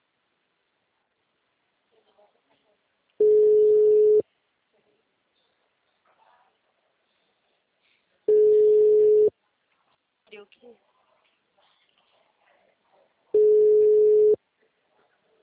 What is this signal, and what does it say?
Ringback tone (Europe, 425 Hz), with speech.